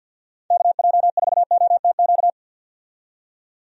Morse code, heard as a keyed tone, about 51 wpm, 700 Hz. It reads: KJ4OTQ
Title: USA Call Signs - 6 Characters - 50wpm